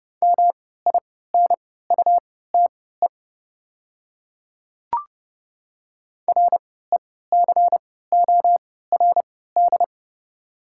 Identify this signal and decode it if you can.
Morse code, heard as a keyed tone, about 30 wpm, spaced as for 17 wpm, 700 Hz. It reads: MINUTE RECORD
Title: 3x Character Spacing - Top 500 Words - 30wpm